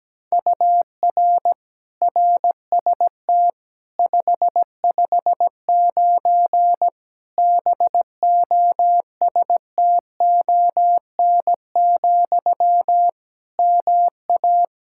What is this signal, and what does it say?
Morse code, heard as a keyed tone, about 17 wpm, 700 Hz. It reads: UR RST 559 BOSTON, MA
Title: Practice QSO - Exchange 1 - Beginning - 17wpm